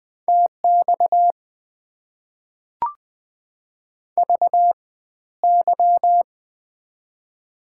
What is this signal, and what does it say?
Morse code, heard as a keyed tone, about 21 wpm, spaced as for 13 wpm, 690 Hz. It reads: TX VY
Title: Top 90 QSO Elements - 4x Character Spacing - 20wpm